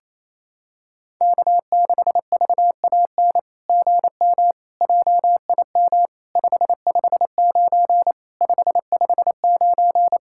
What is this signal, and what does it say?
Morse code, heard as a keyed tone, about 28 wpm, 700 Hz. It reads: K6VAN GM JIM 559 559
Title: SOTA QSO Practice 28wpm